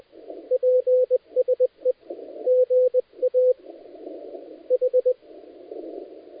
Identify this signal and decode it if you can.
Morse code, heard as a keyed tone, about 20 wpm, 500 Hz. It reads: PSE GA H
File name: dlazaawansowanych / nagraniazpasma / titawaNETcontrol